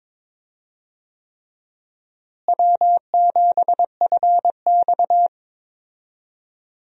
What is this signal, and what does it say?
Morse code, heard as a keyed tone, about 21 wpm, 700 Hz. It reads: W7FX
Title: USA Call Signs - 4 Characters - Repeated 3x - 22wpm